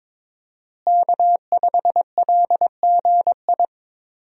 Morse code, heard as a keyed tone, about 22 wpm, 700 Hz. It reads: K5LGI
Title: USA Call Signs - 5 Characters - Repeated 3x - 22wpm